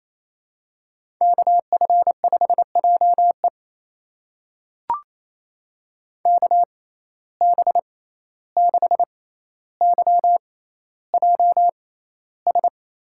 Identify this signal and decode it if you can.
Morse code, heard as a keyed tone, about 28 wpm, spaced as for 14 wpm, 700 Hz. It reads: KF5JE KB6YJS
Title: USA Call Signs - 6x Character Spacing - 28wpm